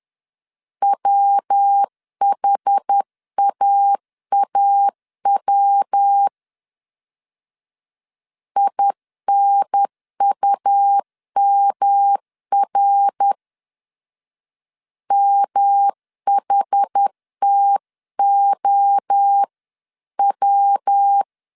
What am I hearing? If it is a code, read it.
Morse code, heard as a keyed tone, about 10 wpm, 800 Hz. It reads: WHAAW INUMR MHTOW
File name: 10wpm